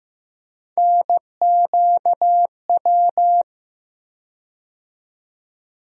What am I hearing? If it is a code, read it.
Morse code, heard as a keyed tone, about 15 wpm, 700 Hz. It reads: NQW
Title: USA Call Sign Postfixes - 3 Letters 15wpm